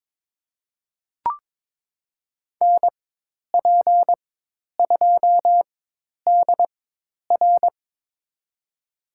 Morse code, heard as a keyed tone, about 22 wpm, spaced as for 10 wpm, 700 Hz. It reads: NP2DR
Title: USA Call Signs - 5 Characters - 4x Character Spacing - 22wpm